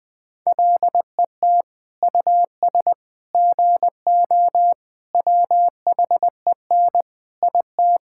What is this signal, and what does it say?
Morse code, heard as a keyed tone, about 21 wpm, 700 Hz. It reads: LET US GO WHEN IT
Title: Sentences with 2-Word Phrases - 20wpm